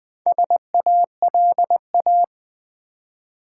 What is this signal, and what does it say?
Morse code, heard as a keyed tone, about 20 wpm, 700 Hz. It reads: SALA
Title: Build A Word - 2 Letters At A Time - 6-Letter Words - 20wpm